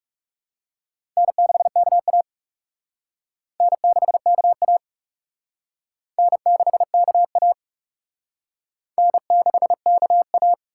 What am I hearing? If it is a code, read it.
Morse code, heard as a keyed tone, about 37 wpm, 700 Hz. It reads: N6KA N6KA N6KA N6KA
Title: Speed Racing - USA 1x2 Call Signs 30wpm